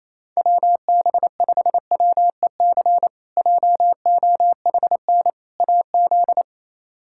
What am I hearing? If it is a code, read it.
Morse code, heard as a keyed tone, about 28 wpm, 700 Hz. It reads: WB5WEC JOHN AZ
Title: K1USN SST QSO Practice - 28wpm